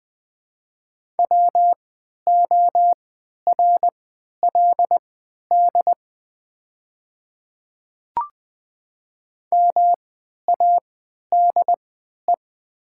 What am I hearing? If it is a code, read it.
Morse code, heard as a keyed tone, about 20 wpm, spaced as for 11 wpm, 700 Hz. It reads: WORLD MADE
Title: 3x Character Spacing - Top 200 Words - 20wpm